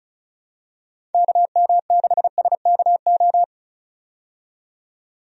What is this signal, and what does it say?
Morse code, heard as a keyed tone, about 36 wpm, 700 Hz. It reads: KM6SKO
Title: USA Call Signs - 6 Characters - 35wpm